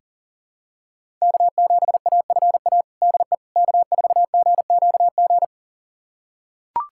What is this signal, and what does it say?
Morse code, heard as a keyed tone, about 40 wpm, 700 Hz. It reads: K7AFA DE K4GQG
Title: USA Callsign DE Callsign - 5 Characters - 40wpm